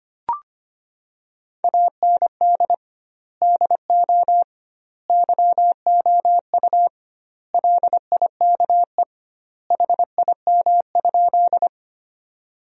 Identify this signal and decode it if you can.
Morse code, heard as a keyed tone, about 25 wpm, 700 Hz. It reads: AND DO YOU LIKE HIM?
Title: 2x Word-Spacing - Sentences from Top 100 Words - 25wpm